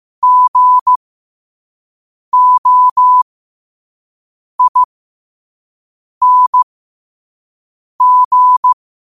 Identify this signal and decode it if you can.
Morse code, heard as a keyed tone, about 15 wpm, spaced as for 5 wpm, 1000 Hz. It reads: GOING
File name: qotdfiles / 5wpm